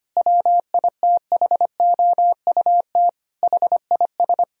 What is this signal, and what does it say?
Morse code, heard as a keyed tone, about 25 wpm, 700 Hz. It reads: WITHOUT HIS
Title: Sentences from Top 1000 Words - Review Part 3 - 25wpm